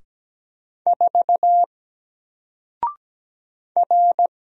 Morse code, heard as a keyed tone, about 17 wpm, 700 Hz. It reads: 4 R
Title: Single Character (Letter:Number:Punctuation:Prosign) Alt 17wpm